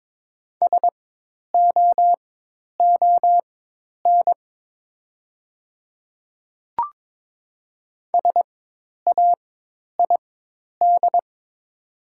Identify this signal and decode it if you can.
Morse code, heard as a keyed tone, about 22 wpm, spaced as for 10 wpm, 700 Hz. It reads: SOON SAID
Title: 4x Character Spacing - Top 500 Words - 22wpm